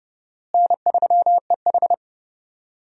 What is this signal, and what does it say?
Morse code, heard as a keyed tone, about 32 wpm, 700 Hz. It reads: N3EH
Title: USA Call Signs - 4 Characters - Repeated 3x - 30wpm